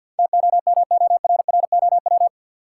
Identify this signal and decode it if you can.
Morse code, heard as a keyed tone, about 52 wpm, 700 Hz. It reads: TOMORROW
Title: Sentences from Top 1100 Words - New Word Drill Alt - 50wpm